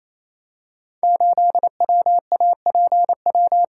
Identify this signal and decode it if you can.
Morse code, heard as a keyed tone, about 29 wpm, 700 Hz. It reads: 8WAPW
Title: Code Groups - 5 Letter:Numbers - 28wpm